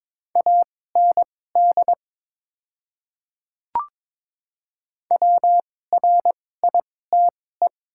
Morse code, heard as a keyed tone, about 22 wpm, spaced as for 16 wpm, 690 Hz. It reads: AND WRITE
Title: 2x Character Spacing - Top 100 Words - 22wpm